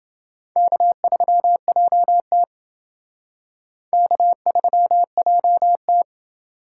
Morse code, heard as a keyed tone, about 28 wpm, 700 Hz. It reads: K3JT K3JT